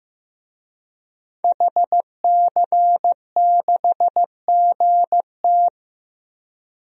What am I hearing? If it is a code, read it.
Morse code, heard as a keyed tone, about 15 wpm, 700 Hz. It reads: HC6GT
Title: DXCC-100 Call Signs - 5 Characters - Repeated 3x - 15wpm